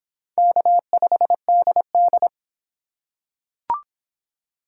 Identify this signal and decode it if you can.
Morse code, heard as a keyed tone, about 25 wpm, 700 Hz. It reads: K5DD